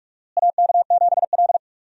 Morse code, heard as a keyed tone, about 46 wpm, 700 Hz. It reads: AK7L
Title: USA Call Signs - 4 Characters - Repeated 3x - 45wpm